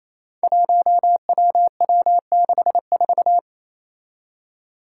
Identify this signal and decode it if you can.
Morse code, heard as a keyed tone, about 28 wpm, 700 Hz. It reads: 1WW64
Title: Five Challenging Letter-Number Combinations 28wpm